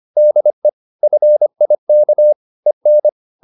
Morse code, heard as a keyed tone, about 25 wpm, 600 Hz. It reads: DE FIK EN
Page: Ferie 25 wpm